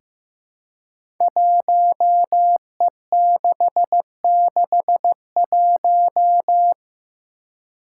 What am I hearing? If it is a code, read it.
Morse code, heard as a keyed tone, about 15 wpm, 700 Hz. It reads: 1E661
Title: Five Challenging Letter-Number Combinations 15wpm